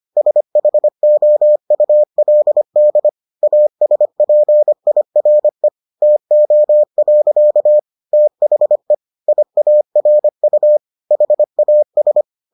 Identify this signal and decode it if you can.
Morse code, heard as a keyed tone, about 25 wpm, 600 Hz. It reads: SHOULD ASPIRE TO. THE IARU HAS